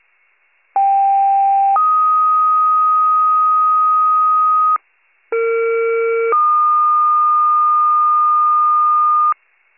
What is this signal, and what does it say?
Two-tone page, heard as a one-second tone, then a three-second tone, about 790 then 1250 Hz, about 460 then 1220 Hz. It is heard from 0.8 to 9.3 s.